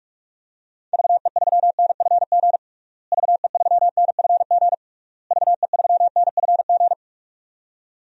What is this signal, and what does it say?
Morse code, heard as a keyed tone, about 41 wpm, 700 Hz. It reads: VE3NFG VE3NFG VE3NFG
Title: DXCC-100 Call Signs - 6 Characters - Repeated 3x - 45wpm